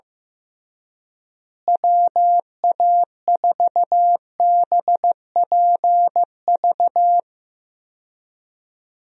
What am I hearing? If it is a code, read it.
Morse code, heard as a keyed tone, about 15 wpm, 700 Hz. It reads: WA4BPV
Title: USA Call Signs - 6 Characters - Repeated 3x - 15wpm